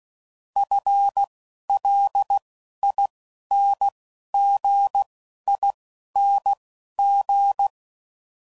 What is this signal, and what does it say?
Morse code, heard as a keyed tone, about 16 wpm, spaced as for 11 wpm, 800 Hz. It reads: FLINGING
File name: Basic Sets / QST de WA7RSO / Combnations